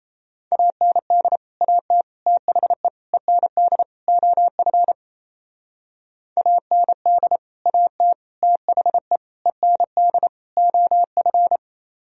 Morse code, heard as a keyed tone, about 34 wpm, 700 Hz. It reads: AND AT THE END OF AND AT THE END OF
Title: Speed-Racing - Sets of 5 Words 28wpm